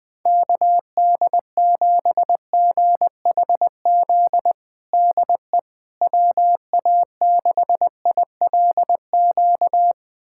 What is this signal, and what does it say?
Morse code, heard as a keyed tone, about 20 wpm, 700 Hz. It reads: KD7GHZ DE WA6ILQ
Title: USA Callsign DE Callsign - 6 Characters - 20wpm